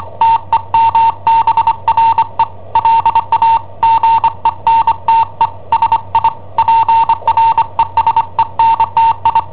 Morse code, heard as a keyed tone, about 25 wpm, 910 Hz. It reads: TEMBRE LA GENTE SI PRESENTI
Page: Cassetta esercizi preparata da I7OHP per IW5ELP